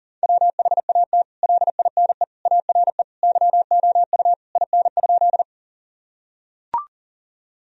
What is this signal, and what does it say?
Morse code, heard as a keyed tone, about 40 wpm, 700 Hz. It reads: WHAT LINE ARE YOU IN?
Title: Sentences from Top 1000 Words - Review Part 1 - 40wpm